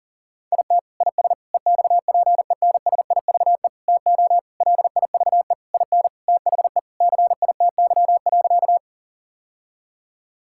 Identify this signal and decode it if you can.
Morse code, heard as a keyed tone, about 40 wpm, 700 Hz. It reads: IT IS EXPENSIVE TO LIVE IN THE CITY.
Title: Sentences from Top 1000 Words - Review Part 6 - 40wpm